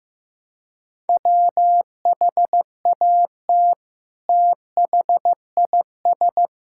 Morse code, heard as a keyed tone, about 16 wpm, 700 Hz. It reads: WHAT THIS
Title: Sentences from Top 1000 Words - Review Part 5 - 15wpm